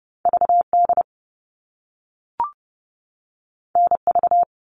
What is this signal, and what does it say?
Morse code, heard as a keyed tone, about 30 wpm, 700 Hz. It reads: VD NV
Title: All Cut Numbers - 2-Digit 30wpm